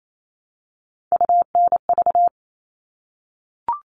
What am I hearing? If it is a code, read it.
Morse code, heard as a keyed tone, about 28 wpm, 700 Hz. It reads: UNV
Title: All Cut Numbers - 3-Digit 28wpm